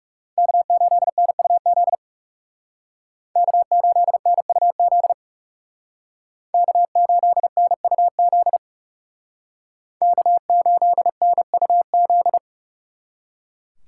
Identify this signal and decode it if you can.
Morse code, heard as a keyed tone, about 38 wpm, 700 Hz. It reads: K8NUZ K8NUZ K8NUZ K8NUZ